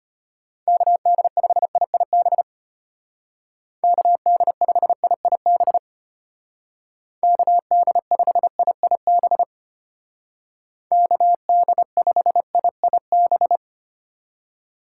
Morse code, heard as a keyed tone, about 31 wpm, 700 Hz. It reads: KD5IIB KD5IIB KD5IIB KD5IIB
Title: Speed Racing - USA 2x3 Call Signs 25wpm